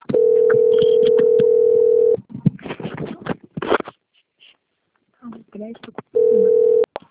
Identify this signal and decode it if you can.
Ringback tone (North America), 440 + 480 Hz, with speech.
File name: out- - - - - .19179